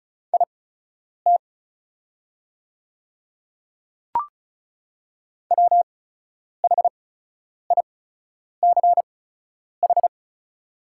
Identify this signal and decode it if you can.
Morse code, heard as a keyed tone, about 36 wpm, spaced as for 10 wpm, 700 Hz. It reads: IT WHICH
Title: 8x Character Spacing - Top 100 Words - 35wpm